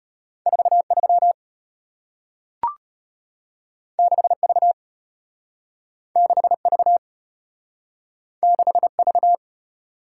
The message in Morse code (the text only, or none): 43 6V 6V 6V